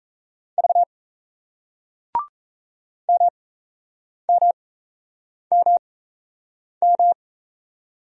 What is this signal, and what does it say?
Morse code, heard as a keyed tone, about 35 wpm, spaced as for 8 wpm, 700 Hz. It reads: V MMMM